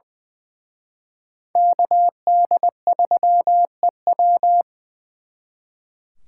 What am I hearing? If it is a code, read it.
Morse code, heard as a keyed tone, about 24 wpm, 700 Hz. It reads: KD3EW
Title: Speed Racing - USA 2x2 Call Signs 20wpm